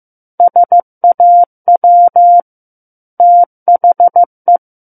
Morse code, heard as a keyed tone, about 15 wpm, 700 Hz. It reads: SAW THE
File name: Book One / 06-Chapter 6 - 15 WPM